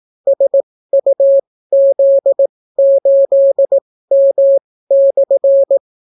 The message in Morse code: SUZ8M/

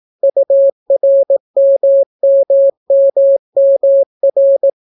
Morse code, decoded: URMMMMR

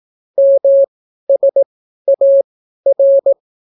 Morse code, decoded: MSAR

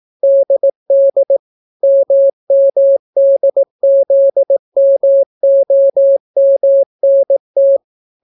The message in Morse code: DD MMDZMOMNT

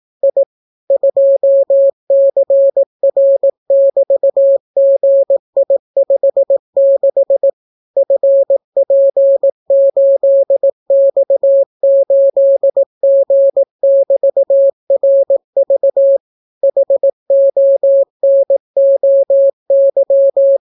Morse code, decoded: I 2CR<BT>GI56 FP8X8G<BT>RV HONOY

